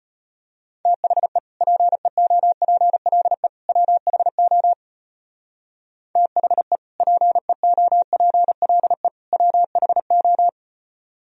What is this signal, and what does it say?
Morse code, heard as a keyed tone, about 30 wpm, 700 Hz. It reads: THE PEOPLE WHO THE PEOPLE WHO
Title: Speed-Racing - Sets of 3 Words 25wpm